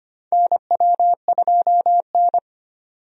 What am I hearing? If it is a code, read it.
Morse code, heard as a keyed tone, about 25 wpm, 700 Hz. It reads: NW2N
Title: USA Call Signs - 4 Characters - 25wpm